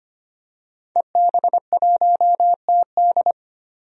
Morse code, heard as a keyed tone, about 25 wpm, 700 Hz. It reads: EB1TD
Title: DXCC-100 Call Signs - 5 Characters - 25wpm